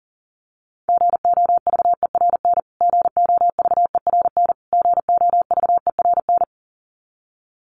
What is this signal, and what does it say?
Morse code, heard as a keyed tone, about 40 wpm, 700 Hz. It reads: GOVERN GOVERN GOVERN
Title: Top 500 Words Repeated 40wpm